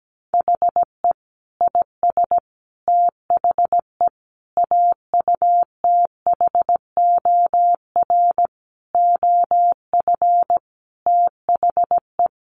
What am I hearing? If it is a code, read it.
Morse code, heard as a keyed tone, about 17 wpm, 700 Hz. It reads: HE IS THE AUTHOR OF THE
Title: Sentences from Top 1200 Words 17wpm